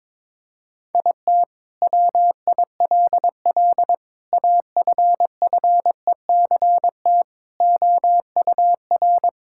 Morse code, heard as a keyed tone, about 22 wpm, 700 Hz. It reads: IT WILL AFFECT OUR
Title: Sentences from Top 1000 Words - Review Part 6 - 22wpm